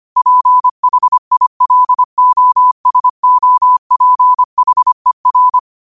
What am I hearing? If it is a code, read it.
Morse code, heard as a keyed tone, about 25 wpm, 1000 Hz. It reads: PHILOSOPHER
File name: qotdfiles / 25wpm